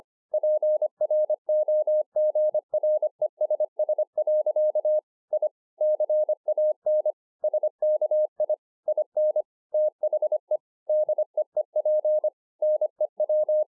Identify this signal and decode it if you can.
Morse code, heard as a keyed tone, about 24 wpm, 610 Hz. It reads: PROGRESS. I CAN SKI IN THE DEEP NEW